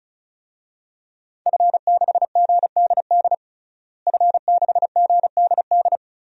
Morse code, heard as a keyed tone, about 36 wpm, 700 Hz. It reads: F6GDD F6GDD
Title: DXCC-100 Call Signs - 5 Characters - Repeated 3x - 35wpm